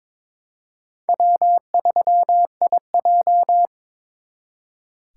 Morse code, heard as a keyed tone, about 26 wpm, 700 Hz. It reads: W3IJ
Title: Speed Racing - USA Call Signs 22wpm